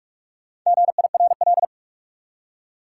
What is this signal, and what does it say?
Morse code, heard as a keyed tone, about 46 wpm, 700 Hz. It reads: GIRL
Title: Four Letter Words - 45wpm